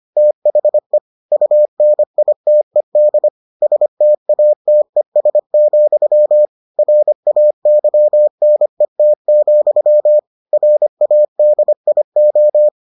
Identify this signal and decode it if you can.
Morse code, heard as a keyed tone, about 25 wpm, 600 Hz. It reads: THE UNITED STATES, RAYNET, RADIO